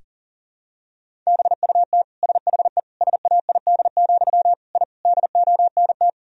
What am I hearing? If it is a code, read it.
Morse code, heard as a keyed tone, about 40 wpm, 700 Hz. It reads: BUT SHE SAID, I DONT
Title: Sentences from Top 1000 Words - Review Part 1 - 40wpm